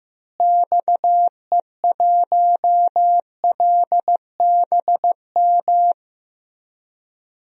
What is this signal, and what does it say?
Morse code, heard as a keyed tone, about 15 wpm, 700 Hz. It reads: XE1LBM
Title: DXCC-100 Call Signs - 6 Characters - Repeated 2x - 15wpm